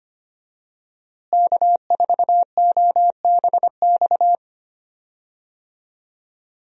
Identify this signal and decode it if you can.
Morse code, heard as a keyed tone, about 25 wpm, 700 Hz. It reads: K4OBX
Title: USA Call Signs - 5 Characters - Repeated 2x - 25wpm